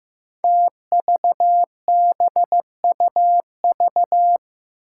Morse code, heard as a keyed tone, about 15 wpm, 700 Hz. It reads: TVBUV
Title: Five Challenging Letter-Number Combinations 15wpm